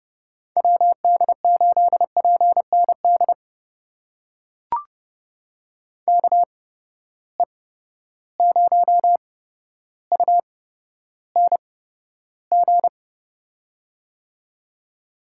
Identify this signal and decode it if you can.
Morse code, heard as a keyed tone, about 30 wpm, spaced as for 13 wpm, 700 Hz. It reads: WD8PND KE0UNG
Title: USA Call Signs - 6 Characters - 8x Character Spacing - 30wpm